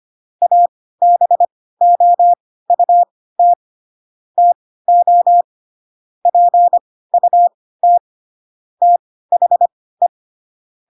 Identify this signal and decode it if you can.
Morse code, heard as a keyed tone, about 25 wpm, spaced as for 16 wpm, 700 Hz. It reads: ABOUT TO PUT THE
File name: Pinocchio / Pinocchio ch3 10 25 ik1hge